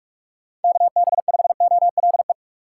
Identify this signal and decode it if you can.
Morse code, heard as a keyed tone, about 47 wpm, 700 Hz. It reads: KB5OLE